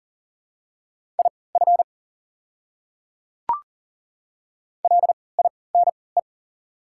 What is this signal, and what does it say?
Morse code, heard as a keyed tone, about 40 wpm, spaced as for 23 wpm, 690 Hz. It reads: IF LINE
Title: 3x Character Spacing - Top 300 Words - 40wpm